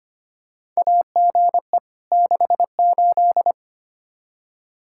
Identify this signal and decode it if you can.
Morse code, heard as a keyed tone, about 25 wpm, 700 Hz. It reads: AGE 68
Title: Practice QSO - Exchange 3 - More Info - 25wpm